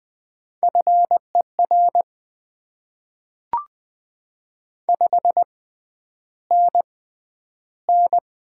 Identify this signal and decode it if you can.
Morse code, heard as a keyed tone, about 20 wpm, spaced as for 10 wpm, 700 Hz. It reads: FER 5NN
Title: Top 20 QSO Elements - 6x Character Spacing - 20wpm